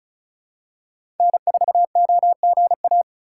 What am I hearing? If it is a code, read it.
Morse code, heard as a keyed tone, about 36 wpm, 700 Hz. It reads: N4OGA
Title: USA Call Signs - 5 Characters - 6x Character Spacing - 35wpm